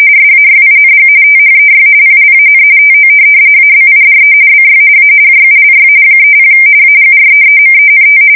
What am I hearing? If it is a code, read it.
RTTY, about 45.45 baud. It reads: SAYING THEY PLAN TO ABIDE BY A JULY 30 AGREEMENT
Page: 170 Shift 2125/2295